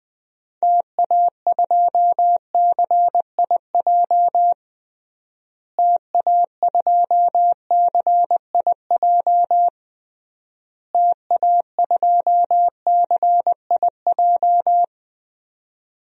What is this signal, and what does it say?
Morse code, heard as a keyed tone, about 20 wpm, 700 Hz. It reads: TA2CIJ TA2CIJ TA2CIJ
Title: DXCC-100 Call Signs - 6 Characters - Repeated 3x - 20wpm